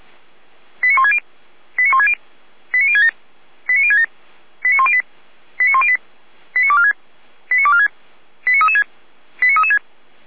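Five-tone selective call (ZVEI), heard as quick runs of five tones, tones of about 70 ms. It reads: 89160 89160 89067 89067 89108 89108 89246 89246 89336 89336
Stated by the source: ZVEI five-tone call 89170 received on an SDR.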